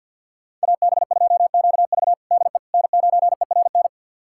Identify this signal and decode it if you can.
Morse code, heard as a keyed tone, about 46 wpm, 700 Hz. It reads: AB2QV DE N9ERN
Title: USA Callsign DE Callsign - 5 Characters - 50wpm